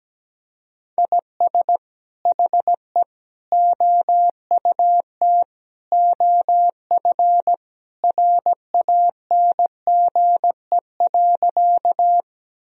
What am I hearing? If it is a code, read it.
Morse code, heard as a keyed tone, about 17 wpm, 700 Hz. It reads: IS HE OUT OF RANGE.
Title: Sentences from Top 1000 Words - Review Part 4 - 17wpm